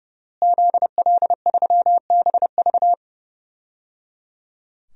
Morse code, heard as a keyed tone, about 30 wpm, 700 Hz. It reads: ZL3BV
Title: DXCC-100 Call Signs - 5 Characters - Repeated 3x - 30wpm